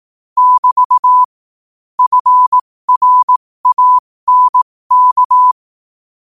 Morse code, heard as a keyed tone, about 18 wpm, spaced as for 15 wpm, 1000 Hz. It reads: <BT> FRANK
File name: qotdfiles / 15wpm